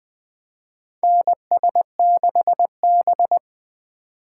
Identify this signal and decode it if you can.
Morse code, heard as a keyed tone, about 20 wpm, 700 Hz. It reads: NS6B
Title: USA Call Signs - 4 Characters - 4x Character Spacing - 20wpm